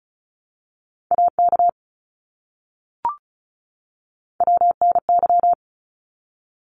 Morse code, heard as a keyed tone, about 34 wpm, 700 Hz. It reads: AK WNY
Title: ARRL Sections 35wpm